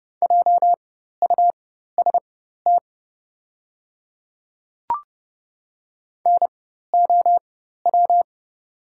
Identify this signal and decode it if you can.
Morse code, heard as a keyed tone, about 30 wpm, spaced as for 14 wpm, 700 Hz. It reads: JUST NOW
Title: 4x Character Spacing - Top 500 Words - 30wpm